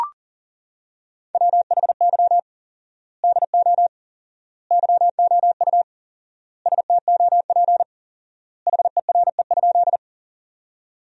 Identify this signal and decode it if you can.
Morse code, heard as a keyed tone, about 40 wpm, 700 Hz. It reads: WHY DO YOU STOP HERE?
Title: 4x Word-Spacing - Sentences from Top 300 Words - 40wpm